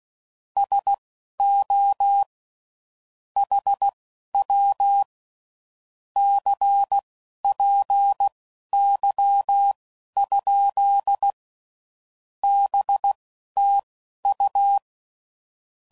Morse code, heard as a keyed tone, about 16 wpm, spaced as for 11 wpm, 800 Hz. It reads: SO HW CPY? BTU
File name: CW Code Groups @10WPM / QSOs / QSO WD0FBT